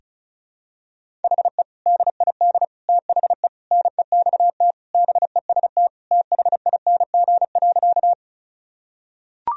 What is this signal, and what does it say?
Morse code, heard as a keyed tone, about 36 wpm, 700 Hz. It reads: HE DID THE NEXT BEST THING.
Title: Sentences from Top 1000 Words - Review Part 1 - 35wpm